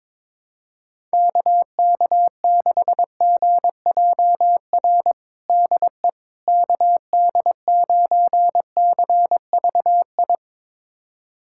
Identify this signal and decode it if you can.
Morse code, heard as a keyed tone, about 22 wpm, 700 Hz. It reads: KK6GJR DE KD9CVI
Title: USA Callsign DE Callsign - 6 Characters - 22wpm